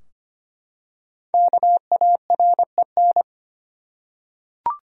Morse code, heard as a keyed tone, about 25 wpm, 700 Hz. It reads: KAREN
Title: USA Common Names - Max 5 Letters - 25wpm